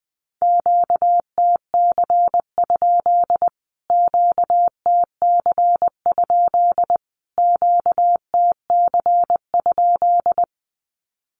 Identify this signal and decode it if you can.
Morse code, heard as a keyed tone, about 20 wpm, 700 Hz. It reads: QTC? QTC? QTC?
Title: Q-Codes Repeated 3x 20wpm